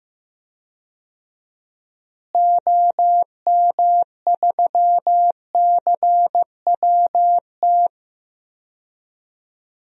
Morse code, heard as a keyed tone, about 16 wpm, 700 Hz. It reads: OM3CWT
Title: DXCC-100 Call Signs - 6 Characters - Repeated 2x - 15wpm